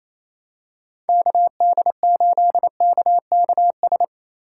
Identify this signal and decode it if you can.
Morse code, heard as a keyed tone, about 28 wpm, 700 Hz. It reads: KD8KKS
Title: USA Call Signs - 6 Characters - 4x Character Spacing - 28wpm